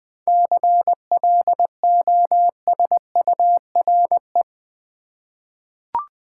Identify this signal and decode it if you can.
Morse code, heard as a keyed tone, about 21 wpm, 690 Hz. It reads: CLOSURE
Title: 7 Letter Words - First 3 Letters Given - 20wpm